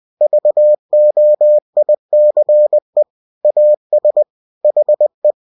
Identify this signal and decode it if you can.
Morse code, heard as a keyed tone, about 20 wpm, 600 Hz. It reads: VOICE AS HE
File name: cw-books / winnie-the-pooh / winnie-20wpm-page-0071